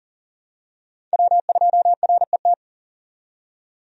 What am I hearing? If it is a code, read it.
Morse code, heard as a keyed tone, about 40 wpm, 700 Hz. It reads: W2RET
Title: USA Call Signs - 5 Characters - 40wpm